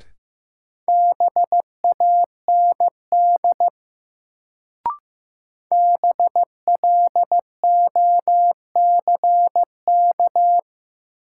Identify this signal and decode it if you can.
Morse code, heard as a keyed tone, about 15 wpm, 700 Hz. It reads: BAND BLOCK
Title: Sentences from Top 1600 Words - New Word Drill - 15wpm